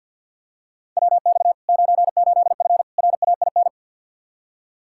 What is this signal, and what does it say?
Morse code, heard as a keyed tone, about 51 wpm, 700 Hz. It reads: WX 98F RAIN
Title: Practice QSO - Exchange 2 - Info - 50wpm